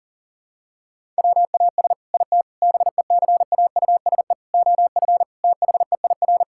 Morse code, heard as a keyed tone, about 40 wpm, 700 Hz. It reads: WAS IT BECAUSE OF THEIR
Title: Sentences from Top 1000 Words - Review Part 6 - 40wpm